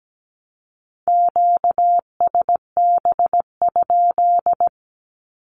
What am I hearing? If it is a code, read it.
Morse code, heard as a keyed tone, about 17 wpm, 700 Hz. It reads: QSB?
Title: Q-Codes Repeated 3x 17wpm